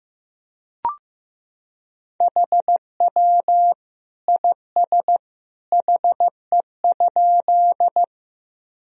Morse code, heard as a keyed tone, about 15 wpm, 700 Hz. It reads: HW IS HE?
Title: Sentences from Top 100 Words and Abbreviations 15wpm